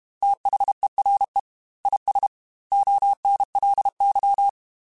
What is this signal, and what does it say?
Morse code, heard as a keyed tone, about 32 wpm, 800 Hz. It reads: THERE IS ONLY
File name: Technical Subjects / Technical Subjects-mp3s / AND Gate @30